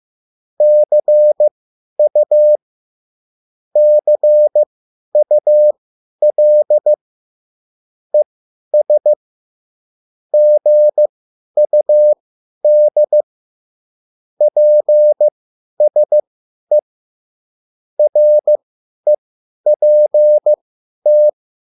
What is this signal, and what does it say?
Morse code, heard as a keyed tone, about 15 wpm, spaced as for 10 wpm, 600 Hz. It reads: CU CUL ES GUD PSE REPT
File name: QSL / CW lesson